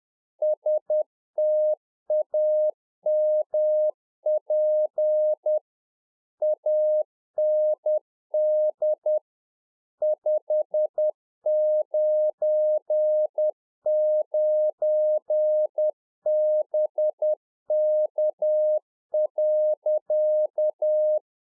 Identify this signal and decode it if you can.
Morse code, heard as a keyed tone, about 10 wpm, 610 Hz. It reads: STAMP AND 599BK.
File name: 2011-04-mp3 / RICK JJ0MPI 10WPM